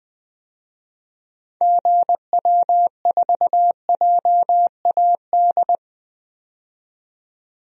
Morse code, heard as a keyed tone, about 20 wpm, 700 Hz. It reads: GW4JAD
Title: DXCC-100 Call Signs - 6 Characters - 20wpm